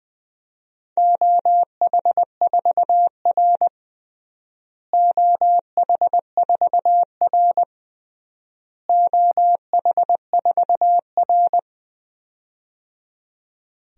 Morse code, heard as a keyed tone, about 21 wpm, 700 Hz. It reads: OH4R OH4R OH4R
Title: DXCC-100 Call Signs - 4 Characters - Repeated 3x - 20wpm